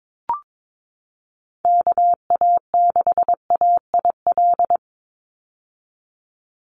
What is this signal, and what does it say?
Morse code, heard as a keyed tone, about 22 wpm, 700 Hz. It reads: KA6AIL
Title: Call Signs - 22wpm